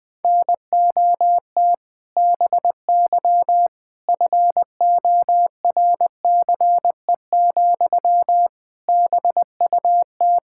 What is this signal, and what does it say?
Morse code, heard as a keyed tone, about 20 wpm, 700 Hz. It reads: NOT BY FORCE, BUT
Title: Sentences from Top 1000 Words - Review Part 5 - 20wpm